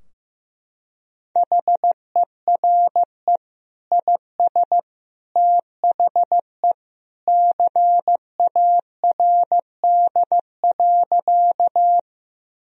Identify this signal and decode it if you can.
Morse code, heard as a keyed tone, about 15 wpm, 700 Hz. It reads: HERE IS THE CARD.
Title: Sentences from Top 1000 Words - Review Part 4 - 15wpm